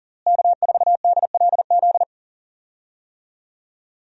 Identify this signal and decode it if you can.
Morse code, heard as a keyed tone, about 38 wpm, 700 Hz. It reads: K4DLZ
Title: USA Call Signs - 5 Characters - Repeated 3x - 40wpm